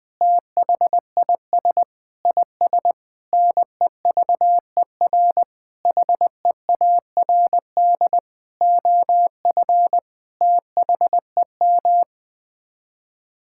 Morse code, heard as a keyed tone, about 20 wpm, 700 Hz. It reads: THIS IS NEVER HEARD OF THEM
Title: Sets of 9 Words - 20wpm